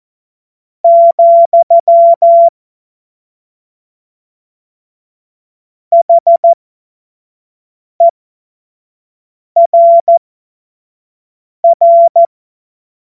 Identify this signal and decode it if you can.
Morse code, heard as a keyed tone, about 14 wpm, spaced as for 5 wpm, 680 Hz. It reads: , HERR
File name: Morsekurs / Witze / Witz 29. Juni 2021, 25 F70